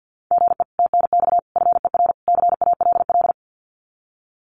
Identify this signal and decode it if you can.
Morse code, heard as a keyed tone, about 50 wpm, 700 Hz. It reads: GE TNX FER CALL
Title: Common QSO Phrases Encore 50wpm